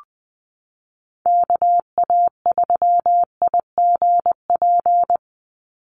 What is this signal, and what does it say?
Morse code, heard as a keyed tone, about 20 wpm, 700 Hz. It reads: KA3IGP
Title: USA Call Signs 20wpm